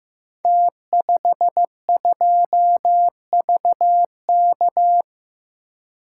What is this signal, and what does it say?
Morse code, heard as a keyed tone, about 15 wpm, 700 Hz. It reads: T52VK
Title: Code Groups - 5 Letter:Numbers - 15wpm